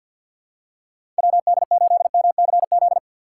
Morse code, heard as a keyed tone, about 52 wpm, 700 Hz. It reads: WD8MCZ
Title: USA Call Signs - 6 Characters - 3x Character Spacing - 50wpm